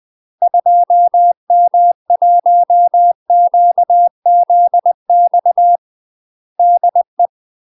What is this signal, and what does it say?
Morse code, heard as a keyed tone, about 20 wpm, 700 Hz. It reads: 2M1QZX DE
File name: UK MASTER QSOs TRAINING/20WPM